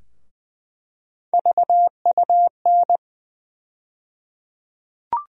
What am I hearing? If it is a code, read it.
Morse code, heard as a keyed tone, about 20 wpm, 700 Hz. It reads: VUN
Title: USA Call Sign Postfixes - 3 Letters 20wpm